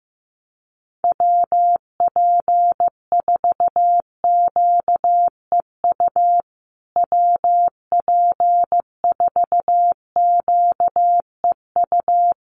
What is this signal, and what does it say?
Morse code, heard as a keyed tone, about 15 wpm, 700 Hz. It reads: WP4QEU WP4QEU
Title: USA Call Signs Repeated 3x 15wpm